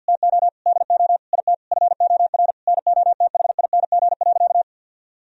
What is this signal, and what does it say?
Morse code, heard as a keyed tone, about 50 wpm, 700 Hz. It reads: TO DO IT FOR NOTHING.
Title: Sentences from Top 1000 Words - Review Part 3 - 50wpm